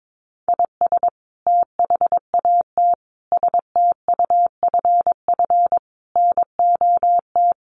Morse code, heard as a keyed tone, about 22 wpm, 700 Hz. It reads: IS THAT STUFF NOT
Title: Sentences from Top 1400 Words 22wpm